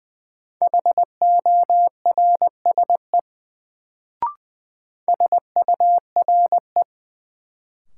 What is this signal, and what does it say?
Morse code, heard as a keyed tone, about 20 wpm, 700 Hz. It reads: HORSE SURE
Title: Sentences from Top 300 Words - New Word Drill - 20wpm